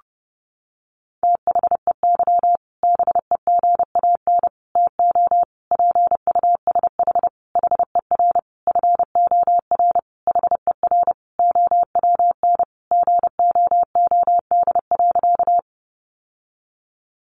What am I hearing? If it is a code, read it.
Morse code, heard as a keyed tone, about 30 wpm, 700 Hz. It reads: THEY BEGAN TO PUSH HER FOR HER OWN GOOD.
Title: Sentences from Top 1300 Words 30wpm